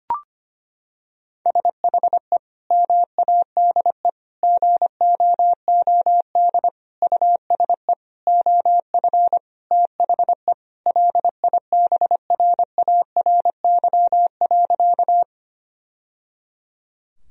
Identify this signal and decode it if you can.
Morse code, heard as a keyed tone, about 25 wpm, 700 Hz. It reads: SHE MADE GOOD USE OF THE LIBRARY.
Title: Sentences from Top 1000 Words - Review Part 4 - 25wpm